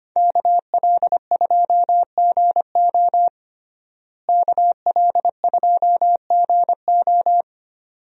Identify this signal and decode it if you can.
Morse code, heard as a keyed tone, about 25 wpm, 700 Hz. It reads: KL2GO KL2GO
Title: USA Call Signs - 5 Characters - Repeated 2x - 25wpm